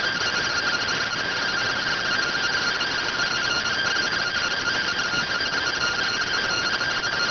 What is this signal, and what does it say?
Two-tone teleprinter signal, about 170 Hz shift.